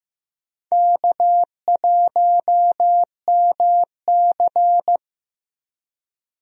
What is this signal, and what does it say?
Morse code, heard as a keyed tone, about 15 wpm, 700 Hz. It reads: K1MC
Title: USA Call Signs - 4 Characters - Repeated 3x - 15wpm